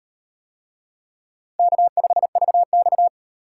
Morse code, heard as a keyed tone, about 36 wpm, 700 Hz. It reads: K5VX